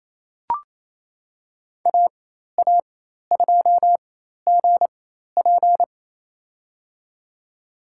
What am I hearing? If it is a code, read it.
Morse code, heard as a keyed tone, about 28 wpm, spaced as for 13 wpm, 700 Hz. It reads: AA2GP
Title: USA Call Signs - 4x Character Spacing - 28wpm